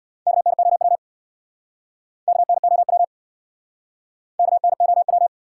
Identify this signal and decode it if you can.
Morse code, heard as a keyed tone, about 62 wpm, 700 Hz. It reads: KNOW KNOW KNOW
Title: Speed-Racing Top 180 QSO Elements 50wpm